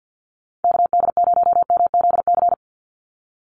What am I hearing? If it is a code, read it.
Morse code, heard as a keyed tone, about 50 wpm, 700 Hz. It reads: KD0MZC
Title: USA Call Signs 50wpm